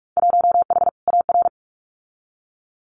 Morse code, heard as a keyed tone, about 46 wpm, 700 Hz. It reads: 1H AR
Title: Winter Field Day Exchanges 45wpm